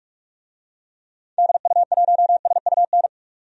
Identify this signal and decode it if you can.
Morse code, heard as a keyed tone, about 46 wpm, 700 Hz. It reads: DU1SUN
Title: DXCC-100 Call Signs - 6 Characters - 45wpm